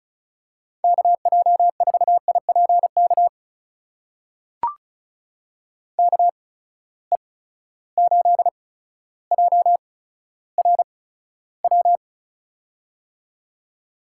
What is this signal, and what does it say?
Morse code, heard as a keyed tone, about 36 wpm, spaced as for 15 wpm, 700 Hz. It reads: KJ4IPK KE8JRW
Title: USA Call Signs - 8x Character Spacing - 35wpm